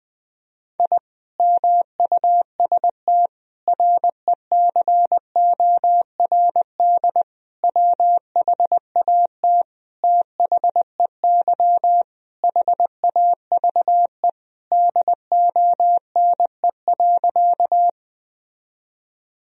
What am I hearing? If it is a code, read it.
Morse code, heard as a keyed tone, about 21 wpm, 700 Hz. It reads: I MUST RECORD WHAT THEY HAVE DONE.
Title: Sentences from Top 1000 Words - Review Part 3 - 20wpm